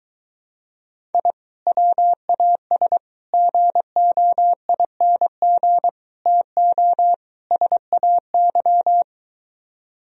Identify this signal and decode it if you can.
Morse code, heard as a keyed tone, about 23 wpm, 700 Hz. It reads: I WAS GOING TO SAY